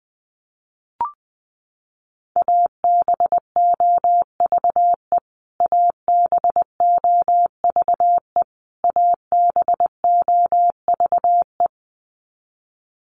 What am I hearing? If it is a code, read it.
Morse code, heard as a keyed tone, about 20 wpm, 700 Hz. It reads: ABOVE ABOVE ABOVE
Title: Top 500 Words Repeated 20wpm